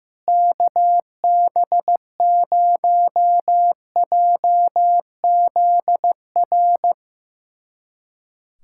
Morse code, heard as a keyed tone, about 15 wpm, 700 Hz. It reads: KB0JZR
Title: USA Callsign DE Callsign - 6 Characters - 15wpm